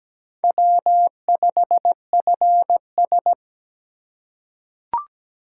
Morse code, heard as a keyed tone, about 17 wpm, 700 Hz. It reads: W5FS
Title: USA Call Signs - 4 Characters - 2x Character Spacing - 17wpm